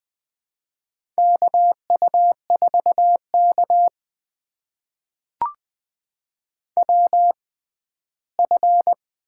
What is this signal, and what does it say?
Morse code, heard as a keyed tone, about 20 wpm, spaced as for 14 wpm, 700 Hz. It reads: KU4K WF
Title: USA Call Signs - 4 Characters - 6x Character Spacing - 20wpm